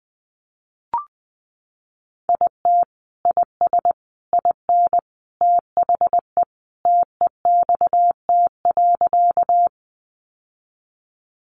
Morse code, heard as a keyed tone, about 20 wpm, 700 Hz. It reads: IT IS IN THE TEXT.
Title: Sentences from Top 1200 Words 20wpm